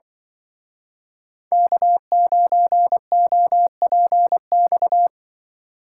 Morse code, heard as a keyed tone, about 24 wpm, 700 Hz. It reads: K9OPX
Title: Speed Racing - USA 1x3 Call Signs 20wpm